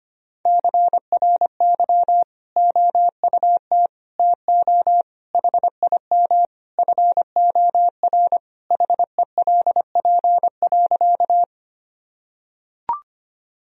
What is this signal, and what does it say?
Morse code, heard as a keyed tone, about 25 wpm, 700 Hz. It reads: CRY OUT TO HIM FOR HELP.
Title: Sentences from Top 1000 Words - Review Part 2 - 25wpm